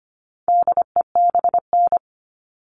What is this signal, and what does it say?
Morse code, heard as a keyed tone, about 25 wpm, 700 Hz. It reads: DEBN
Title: All Cut Numbers - 4-Digit 25wpm